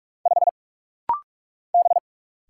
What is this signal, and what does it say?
Morse code, heard as a keyed tone, about 44 wpm, 700 Hz. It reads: 5 B
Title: Single Letter-Number Alt - Rapid-Fire 45wpm